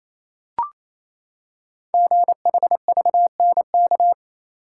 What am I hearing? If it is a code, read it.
Morse code, heard as a keyed tone, about 27 wpm, 700 Hz. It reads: GHVNK
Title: Code Groups - 5 Letters - 28wpm